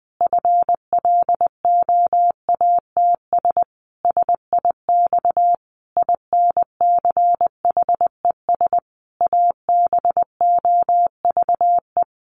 Morse code, heard as a keyed tone, about 20 wpm, 700 Hz. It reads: FLOATS SIX INCHES ABOVE
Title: A Sound Of Thunder 20wpm